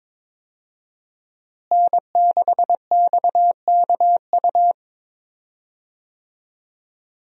Morse code, heard as a keyed tone, about 21 wpm, 700 Hz. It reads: N6XKU
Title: USA Call Signs - 5 Characters - Repeated 3x - 22wpm